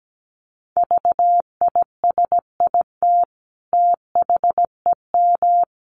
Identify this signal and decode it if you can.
Morse code, heard as a keyed tone, about 17 wpm, 700 Hz. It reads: VISIT THEM
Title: Sentences from Top 900 Words 17wpm